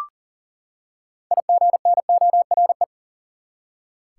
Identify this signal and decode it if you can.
Morse code, heard as a keyed tone, about 40 wpm, 700 Hz. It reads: IGNORE
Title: Six Letter Words - 40wpm